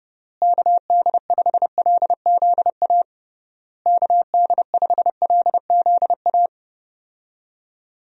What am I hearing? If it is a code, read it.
Morse code, heard as a keyed tone, about 32 wpm, 700 Hz. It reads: KD5LZA KD5LZA
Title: USA Call Signs - 6 Characters - Repeated 3x - 30wpm